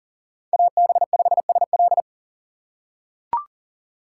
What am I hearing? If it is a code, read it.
Morse code, heard as a keyed tone, about 40 wpm, 700 Hz. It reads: AB5SL
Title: USA Call Signs - 5 Characters - 4x Character Spacing - 40wpm